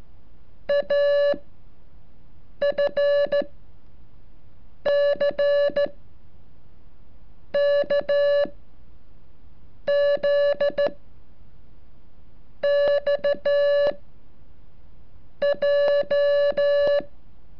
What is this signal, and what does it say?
Morse code, heard as a keyed tone, about 13 wpm, spaced as for 5 wpm, 570 Hz. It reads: AFCKZXJ
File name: curso / ejercicio 37